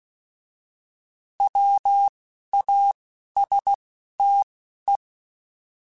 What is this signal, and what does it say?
Morse code, heard as a keyed tone, about 16 wpm, 800 Hz. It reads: W A S T E
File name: Alphabetical / Alphabetical mp3s / Ws @10